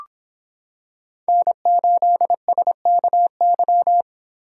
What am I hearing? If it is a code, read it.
Morse code, heard as a keyed tone, about 25 wpm, 700 Hz. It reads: N8SKY